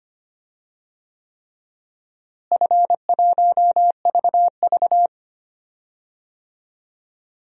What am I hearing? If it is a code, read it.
Morse code, heard as a keyed tone, about 25 wpm, 700 Hz. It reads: F1VV
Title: DXCC-100 Call Signs - 4 Characters - Repeated 3x - 25wpm